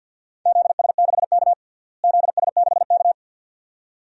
Morse code, heard as a keyed tone, about 49 wpm, 700 Hz. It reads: ZS6K ZS6K
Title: DXCC-100 Call Signs - 4 Characters - Repeated 2x - 50wpm